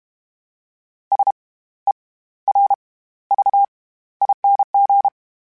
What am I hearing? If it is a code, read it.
Morse code, heard as a keyed tone, about 32 wpm, spaced as for 16 wpm, 800 Hz. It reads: SERVING
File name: Special INGs / Special INGs-mp3s / SlowToFaster ing @30